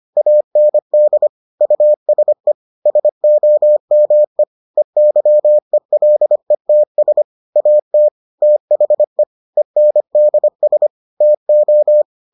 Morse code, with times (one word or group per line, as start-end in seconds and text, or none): 0.2-1.3 AND
1.6-2.5 USE
2.9-4.4 SOME
4.8-7.2 EYELETS
7.6-8.1 AT
8.4-9.2 THE
9.6-10.9 ENDS
11.2-12.0 TO